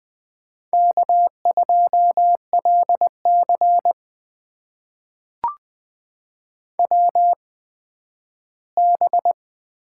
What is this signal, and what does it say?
Morse code, heard as a keyed tone, about 20 wpm, spaced as for 12 wpm, 700 Hz. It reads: K2LC WB
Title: USA Call Signs - 8x Character Spacing - 20wpm